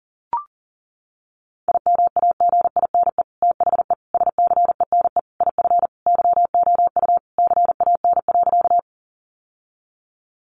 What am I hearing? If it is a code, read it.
Morse code, heard as a keyed tone, about 40 wpm, 700 Hz. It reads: IMAGINE THE SCENE IF YOU CAN.
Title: Sentences from Top 1400 Words 40wpm